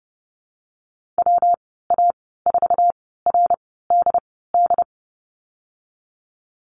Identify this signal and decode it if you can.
Morse code, heard as a keyed tone, about 32 wpm, spaced as for 17 wpm, 700 Hz. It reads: WA4RDD
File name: Lesson 19g - Letter w - Call Signs 30wpm